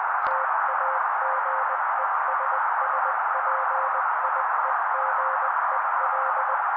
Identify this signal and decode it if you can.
Morse code, heard as a keyed tone, about 20 wpm, 530 Hz. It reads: TAGESSPIEGEL
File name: daily / 18wpm noise